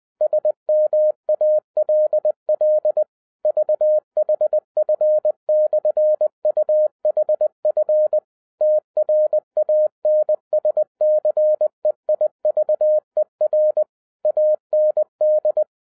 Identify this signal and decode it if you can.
Morse code, heard as a keyed tone, about 20 wpm, 610 Hz. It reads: SMALL VHF/UHF TRANSCEIVER AND
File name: 2012-01-mp3 / n05-CAME BACK TO HAM RADIO AFTER 30 YEARS OF INTERVAL-20wpm